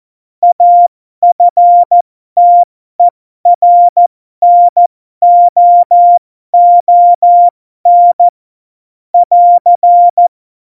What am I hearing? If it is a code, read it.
Morse code, heard as a keyed tone, about 14 wpm, spaced as for 12 wpm, 700 Hz. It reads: AFTERNOON <AR>
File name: Morsekurs / Witze